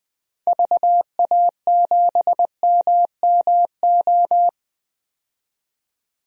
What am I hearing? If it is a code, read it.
Morse code, heard as a keyed tone, about 20 wpm, 700 Hz. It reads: VA7MMO
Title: DXCC-100 Call Signs - 6 Characters - 20wpm